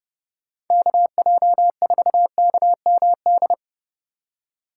Morse code, heard as a keyed tone, about 30 wpm, 700 Hz. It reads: KJ4KMD
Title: USA Call Signs - 2x Character Spacing - 30wpm